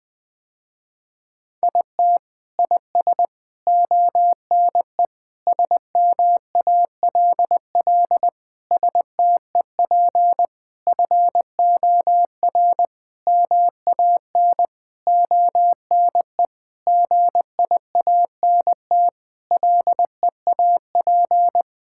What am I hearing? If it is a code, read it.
Morse code, heard as a keyed tone, about 21 wpm, 700 Hz. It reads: IT IS ONE SMALL STEP FOR MAN ONE GIANT LEAP
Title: Sentences with 9-Word Phrases - Spoken Beginning - 20wpm